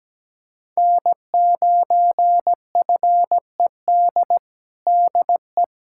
Morse code, heard as a keyed tone, about 17 wpm, 700 Hz. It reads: N9FED DE
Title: USA Callsign DE Callsign - 5 Characters - 17wpm